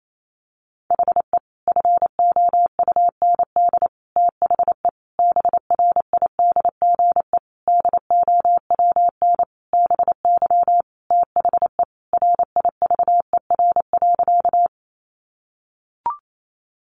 Morse code, heard as a keyed tone, about 28 wpm, 700 Hz. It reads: HE FOUND THE BRIDGE DOWN BY THE RIVER.
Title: Sentences from Top 1200 Words 28wpm